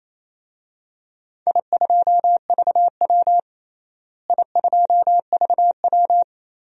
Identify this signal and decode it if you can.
Morse code, heard as a keyed tone, about 28 wpm, 700 Hz. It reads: I2VW I2VW
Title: DXCC-100 Call Signs - 4 Characters - Repeated 3x - 28wpm